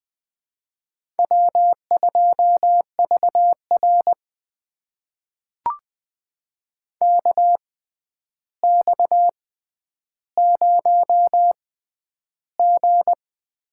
Morse code, heard as a keyed tone, about 20 wpm, spaced as for 10 wpm, 700 Hz. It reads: W2VR KX0G
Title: USA Call Signs - 4 Characters - 6x Character Spacing - 20wpm